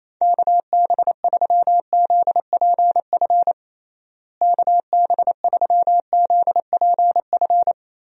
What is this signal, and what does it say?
Morse code, heard as a keyed tone, about 28 wpm, 700 Hz. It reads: KB3ZPF KB3ZPF
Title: USA Call Signs - 6 Characters - Repeated 3x - 28wpm